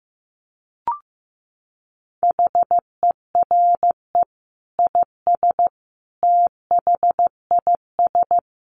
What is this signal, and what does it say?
Morse code, heard as a keyed tone, about 15 wpm, 700 Hz. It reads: HERE IS THIS
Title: Sentences from Top 1000 Words 15wpm